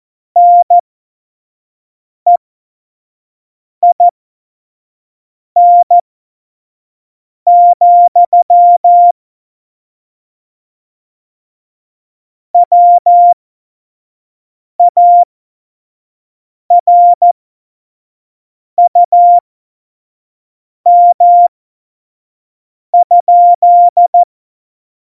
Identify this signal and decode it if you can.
Morse code, heard as a keyed tone, about 14 wpm, spaced as for 5 wpm, 700 Hz. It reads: NEIN, WARUM?
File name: Morsekurs / Witze / Witz 15. Oktober 2019, 25 F70